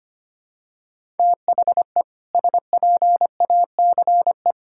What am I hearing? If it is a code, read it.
Morse code, heard as a keyed tone, about 25 wpm, 700 Hz. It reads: THE SPACE
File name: practice / 25wpm / All-Age Appropriate Jokes - 25wpm